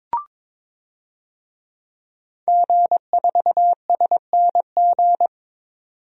Morse code, heard as a keyed tone, about 22 wpm, 700 Hz. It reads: G4SNG
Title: DXCC-100 Call Signs - 5 Characters - 22wpm